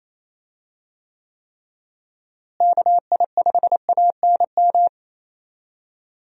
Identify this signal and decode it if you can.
Morse code, heard as a keyed tone, about 28 wpm, 700 Hz. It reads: KI5ANM
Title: USA Call Signs - 6 Characters - Repeated 3x - 28wpm